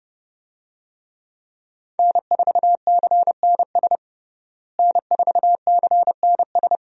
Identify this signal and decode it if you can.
Morse code, heard as a keyed tone, about 32 wpm, 700 Hz. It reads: N4CNS N4CNS
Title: USA Call Signs - 5 Characters - Repeated 2x - 30wpm